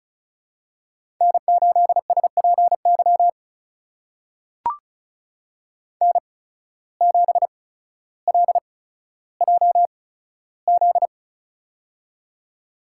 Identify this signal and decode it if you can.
Morse code, heard as a keyed tone, about 36 wpm, spaced as for 15 wpm, 700 Hz. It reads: N8SPY N7LJZ
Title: USA Call Signs - 5 Characters - 8x Character Spacing - 35wpm